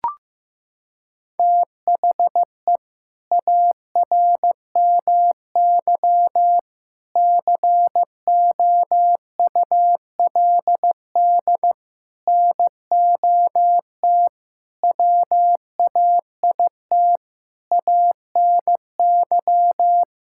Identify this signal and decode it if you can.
Morse code, heard as a keyed tone, about 15 wpm, 700 Hz. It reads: THE ARMY COULD NOT WAIT ANY
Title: Sentences from Top 1000 Words - Review Part 5 - 15wpm